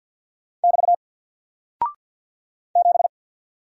Morse code, heard as a keyed tone, about 50 wpm, 700 Hz. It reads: <BT> 7
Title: Single Character (Letter:Number:Punctuation:Prosign) 50wpm